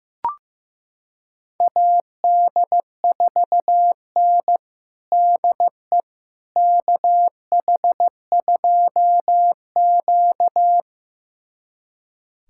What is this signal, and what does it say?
Morse code, heard as a keyed tone, about 15 wpm, 700 Hz. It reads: AD4N DE KH2Q
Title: USA Callsign DE Callsign - 4 Characters - 15wpm